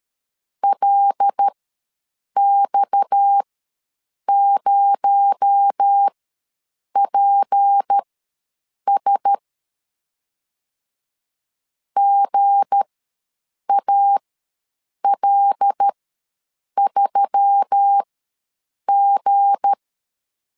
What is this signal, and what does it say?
Morse code, heard as a keyed tone, about 13 wpm, spaced as for 7 wpm, 800 Hz. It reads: LX0PS GAL3G
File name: LEZIONI / 7 GF9Z / 7 G ws 12wpm 3140